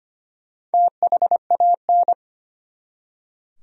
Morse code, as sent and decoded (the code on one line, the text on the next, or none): - .... .- -.
THAN